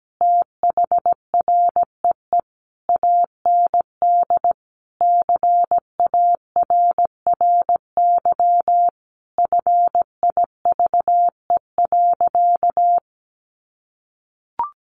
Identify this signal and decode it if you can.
Morse code, heard as a keyed tone, about 17 wpm, 700 Hz. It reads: THREE AND CARRY FIVE.
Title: Sentences from Top 400 Words 17wpm